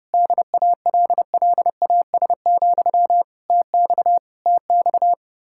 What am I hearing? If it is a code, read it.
Morse code, heard as a keyed tone, about 30 wpm, 700 Hz. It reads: DALLAS, TX TX
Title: Practice QSO - Exchange 1 - Beginning - 30wpm